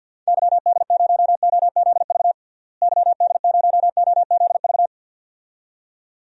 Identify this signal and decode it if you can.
Morse code, heard as a keyed tone, about 47 wpm, 700 Hz. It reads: YD0OZV YD0OZV
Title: DXCC-100 Call Signs - 6 Characters - Repeated 2x - 50wpm